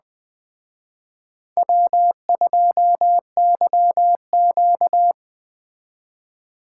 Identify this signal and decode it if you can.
Morse code, heard as a keyed tone, about 20 wpm, 700 Hz. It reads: W2YQ
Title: Speed Racing - USA 1x2 Call Signs 17wpm